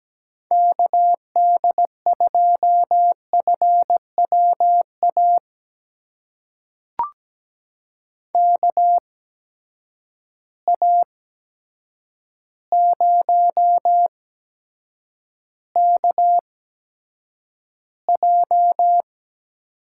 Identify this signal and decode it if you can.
Morse code, heard as a keyed tone, about 17 wpm, spaced as for 8 wpm, 700 Hz. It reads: KD2FWA KA0KJ
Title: USA Call Signs - 6 Characters - 8x Character Spacing - 17wpm